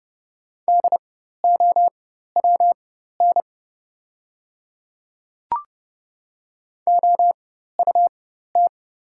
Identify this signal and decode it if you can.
Morse code, heard as a keyed tone, about 30 wpm, spaced as for 14 wpm, 700 Hz. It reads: DOWN OUT
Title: 4x Character Spacing - Top 100 Words - 30wpm